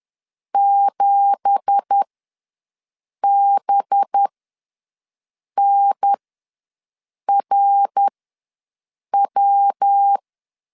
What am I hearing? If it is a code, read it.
Morse code, heard as a keyed tone, about 10 wpm, spaced as for 5 wpm, 800 Hz. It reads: 7BNRW
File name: LEZIONI / 10 SIGNS / 10C DOT ws 10wpm 4213